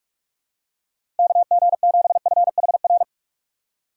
Morse code, heard as a keyed tone, about 45 wpm, 700 Hz. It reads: KG7FHR
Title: USA Call Signs - 8x Character Spacing - 45wpm